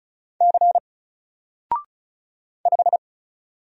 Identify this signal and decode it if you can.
Morse code, heard as a keyed tone, about 36 wpm, 700 Hz. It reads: C 5
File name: Single-Letter-Number-35wpm